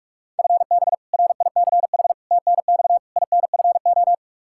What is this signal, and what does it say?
Morse code, heard as a keyed tone, about 42 wpm, 700 Hz. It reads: FB RICH TNX INFO
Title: Practice QSO - Exchange 3 - More Info - 45wpm